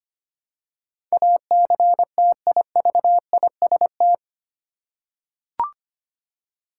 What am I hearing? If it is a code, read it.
Morse code, heard as a keyed tone, about 25 wpm, 700 Hz. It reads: ACTIVIST
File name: Eight Letter Words - 25wpm